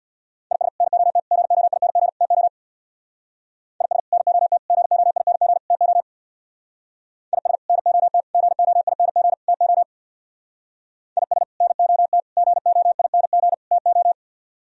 Morse code, heard as a keyed tone, about 64 wpm, 700 Hz. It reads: IS NOT GOING TO IS NOT GOING TO IS NOT GOING TO IS NOT GOING TO